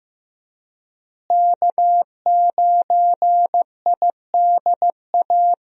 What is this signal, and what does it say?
Morse code, heard as a keyed tone, about 15 wpm, 700 Hz. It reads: K9IDA
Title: USA Call Signs - 5 Characters - Repeated 3x - 15wpm